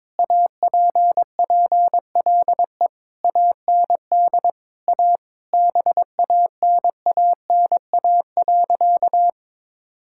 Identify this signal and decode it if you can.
Morse code, heard as a keyed tone, about 22 wpm, 700 Hz. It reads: APPLE AND A BANANA.
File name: practice / 22wpm / Sentences with 2-Word Phrases - 22wpm